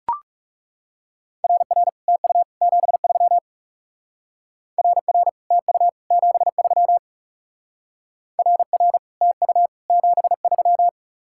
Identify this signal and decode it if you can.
Morse code, heard as a keyed tone, about 40 wpm, 700 Hz. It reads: RR TU 73 RR TU 73 RR TU 73